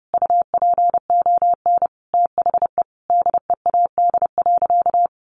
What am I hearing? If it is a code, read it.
Morse code, heard as a keyed tone, about 30 wpm, 700 Hz. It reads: UPON THE DEAD.
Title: Sentences from Top 1500 Words 30wpm